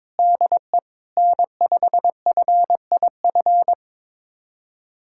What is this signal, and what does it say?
Morse code, heard as a keyed tone, about 22 wpm, 700 Hz. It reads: DE N5FIF
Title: USA Callsign DE Callsign 22wpm